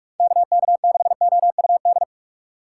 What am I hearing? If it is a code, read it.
Morse code, heard as a keyed tone, about 46 wpm, 700 Hz. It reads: KK6OUD
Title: USA Call Signs - 6 Characters - 6x Character Spacing - 45wpm